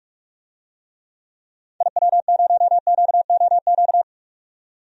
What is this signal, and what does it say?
Morse code, heard as a keyed tone, about 45 wpm, 700 Hz. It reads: IW0QOQ
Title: DXCC-100 Call Signs - 6 Characters - 45wpm